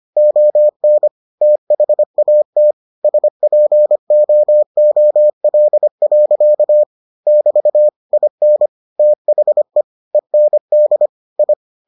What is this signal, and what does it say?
Morse code, heard as a keyed tone, about 25 wpm, 600 Hz. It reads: ON THAT SPOOL. <BT> IN THE END I